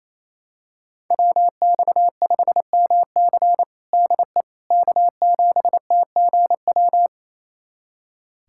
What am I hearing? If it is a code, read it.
Morse code, heard as a keyed tone, about 28 wpm, 700 Hz. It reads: WX5MC DE K7TGW
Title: USA Callsign DE Callsign - 5 Characters - 28wpm